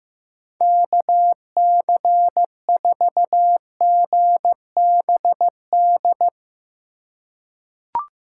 Morse code, heard as a keyed tone, about 15 wpm, 700 Hz. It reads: KC4GBD
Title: USA Call Signs - 6 Characters - 4x Character Spacing - 15wpm